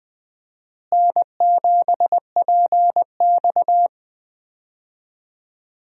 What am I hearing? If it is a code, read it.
Morse code, heard as a keyed tone, about 20 wpm, 700 Hz. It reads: N7PX
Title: USA Call Signs - 4 Characters - Repeated 2x - 20wpm